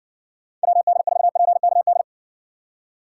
Morse code, heard as a keyed tone, about 58 wpm, 700 Hz. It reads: WD4PKD